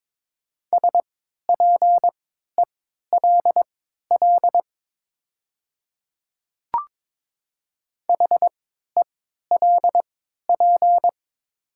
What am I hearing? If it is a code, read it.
Morse code, heard as a keyed tone, about 22 wpm, spaced as for 12 wpm, 700 Hz. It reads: SPELL HELP
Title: 3x Character Spacing - Top 200 Words - 22wpm